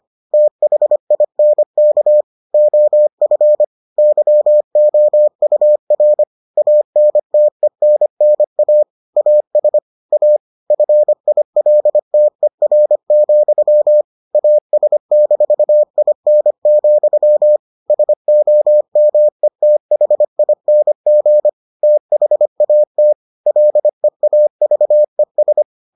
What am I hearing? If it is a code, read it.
Morse code, heard as a keyed tone, about 25 wpm, 600 Hz. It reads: THINK OF YOUR ANTENNA AS A FILTER, AS-IN, SOMETHING THAT LEAVES